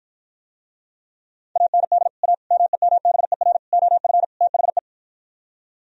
Morse code, heard as a keyed tone, about 52 wpm, 700 Hz. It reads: AND A MEMBER OF THE